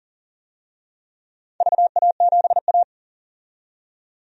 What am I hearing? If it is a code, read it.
Morse code, heard as a keyed tone, about 42 wpm, 700 Hz. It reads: VA7A
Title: DXCC-100 Call Signs - 4 Characters - 40wpm